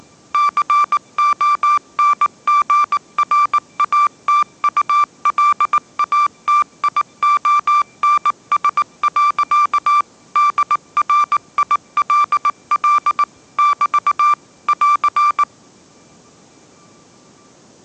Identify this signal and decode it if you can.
Morse code, heard as a keyed tone, about 19 wpm, 1180 Hz. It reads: CONGRATULATIONS. DRILL <BT> <AR>
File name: mrx083 20wpm